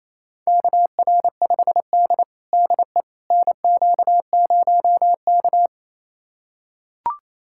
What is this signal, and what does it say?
Morse code, heard as a keyed tone, about 27 wpm, 700 Hz. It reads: KR5D DE NQ0K
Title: USA Callsign DE Callsign - 4 Characters - 28wpm